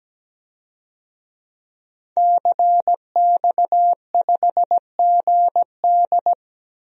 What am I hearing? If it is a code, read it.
Morse code, heard as a keyed tone, about 17 wpm, 700 Hz. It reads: CX5GD
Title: DXCC-100 Call Signs - 5 Characters - Repeated 2x - 17wpm